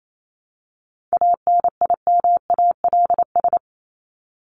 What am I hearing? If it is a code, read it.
Morse code, heard as a keyed tone, about 28 wpm, 700 Hz. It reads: ANIMALS
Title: ES:S Suffix - Top 100 Words with Base Words 28wpm